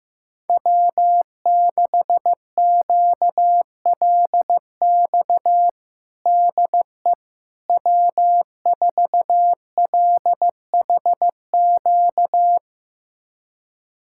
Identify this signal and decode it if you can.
Morse code, heard as a keyed tone, about 16 wpm, 700 Hz. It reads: W6QLX DE W4LHQ
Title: USA Callsign DE Callsign - 5 Characters - 15wpm